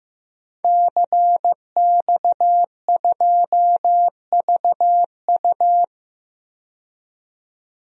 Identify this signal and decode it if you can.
Morse code, heard as a keyed tone, about 16 wpm, 700 Hz. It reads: CX2VU
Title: DXCC-100 Call Signs - 5 Characters - 15wpm